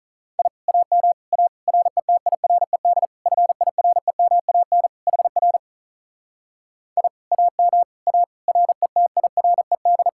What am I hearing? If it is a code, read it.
Morse code, heard as a keyed tone, about 46 wpm, 700 Hz. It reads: I AM A RETIRED FIREMAN HR I AM A RETIRED